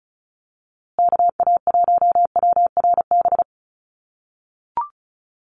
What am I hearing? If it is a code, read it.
Morse code, heard as a keyed tone, about 34 wpm, 700 Hz. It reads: KA1WRB
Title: USA Call Signs Repeated 3x 35wpm